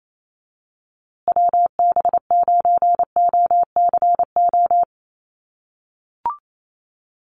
Morse code, heard as a keyed tone, about 28 wpm, 700 Hz. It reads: WB9OCO
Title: USA Call Signs Repeated 3x 28wpm